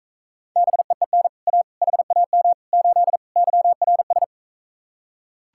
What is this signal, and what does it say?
Morse code, heard as a keyed tone, about 45 wpm, 700 Hz. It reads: BEEN A HAM 8 YRS